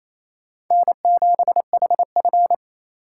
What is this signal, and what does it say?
Morse code, heard as a keyed tone, about 35 wpm, 700 Hz. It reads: N7HF
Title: Speed Racing - USA 1x2 Call Signs 28wpm